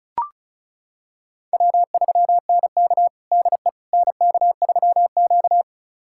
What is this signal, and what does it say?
Morse code, heard as a keyed tone, about 36 wpm, 700 Hz. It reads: W3NK DE NK3Q
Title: USA Callsign DE Callsign - 4 Characters - 35wpm